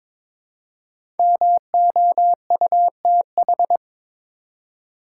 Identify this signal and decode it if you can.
Morse code, heard as a keyed tone, about 22 wpm, 700 Hz. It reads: MOUTH
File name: 5 Letter Words - First 3 Letters Given - 22wpm